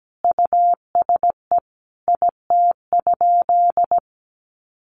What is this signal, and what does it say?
Morse code, heard as a keyed tone, about 17 wpm, 700 Hz. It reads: USE IT?
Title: Sentences from Top 200 Words 17wpm